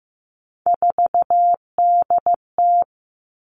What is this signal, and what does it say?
Morse code, heard as a keyed tone, about 15 wpm, 700 Hz. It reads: 4DT
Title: Three Challenging Letter-Number Combinations 15wpm